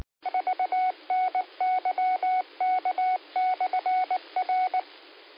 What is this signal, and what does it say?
Morse code, heard as a keyed tone, about 19 wpm, 740 Hz. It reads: VNYK/R